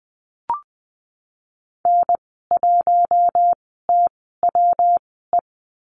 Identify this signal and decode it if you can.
Morse code, heard as a keyed tone, about 20 wpm, spaced as for 14 wpm, 700 Hz. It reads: N1TWE
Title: 140 Lesson 23g - Letter f - Call Signs 20wpm